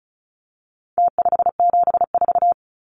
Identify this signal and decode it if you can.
Morse code, heard as a keyed tone, about 34 wpm, 700 Hz. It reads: T574
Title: Numbers - Contest Style - 4-Digit 35wpm